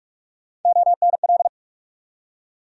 Morse code, heard as a keyed tone, about 46 wpm, 700 Hz. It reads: ONL
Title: USA Call Sign Postfixes - 3 Letters 45wpm